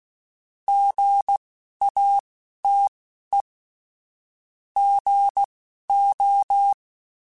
Morse code, heard as a keyed tone, about 16 wpm, spaced as for 12 wpm, 800 Hz. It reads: GATE GO
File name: Technical Subjects / Technical Subjects-mp3s / OR Gate @10